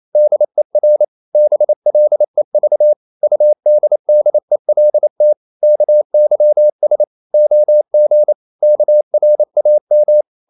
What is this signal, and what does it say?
Morse code, heard as a keyed tone, about 28 wpm, 600 Hz. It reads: DER BLEV UDDELT KYS OG KRAM
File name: Ferie QRQ 28wpm